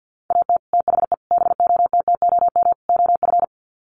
Morse code, heard as a keyed tone, about 48 wpm, 700 Hz. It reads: AT THE BOTTOM OF
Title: Sets of 4 Words 50wpm